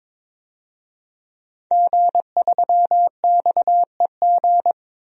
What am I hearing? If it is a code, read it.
Morse code, heard as a keyed tone, about 22 wpm, 700 Hz. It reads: G3XEG
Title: DXCC-100 Call Signs - 5 Characters - Repeated 3x - 22wpm